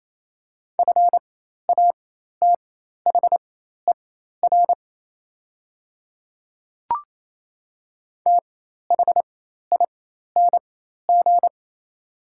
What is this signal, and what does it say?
Morse code, heard as a keyed tone, about 27 wpm, spaced as for 13 wpm, 690 Hz. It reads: FATHER THING
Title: 4x Character Spacing - Top 200 Words - 28wpm